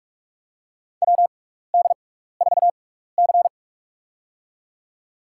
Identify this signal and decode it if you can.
Morse code, heard as a keyed tone, about 46 wpm, spaced as for 16 wpm, 700 Hz. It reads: WD4C
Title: USA Call Signs - 4 Characters - 6x Character Spacing - 45wpm